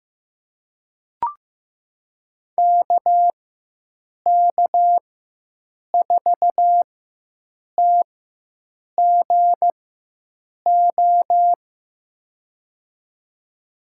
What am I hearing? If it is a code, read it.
Morse code, heard as a keyed tone, about 15 wpm, spaced as for 7 wpm, 700 Hz. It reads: KK4TGO
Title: USA Call Signs - 4x Character Spacing - 15wpm